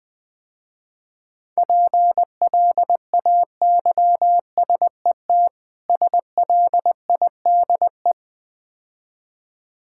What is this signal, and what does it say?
Morse code, heard as a keyed tone, about 21 wpm, 700 Hz. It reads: PLAYSET SLIDE
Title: Difficult Words - Speed-Racing with Hints - 20wpm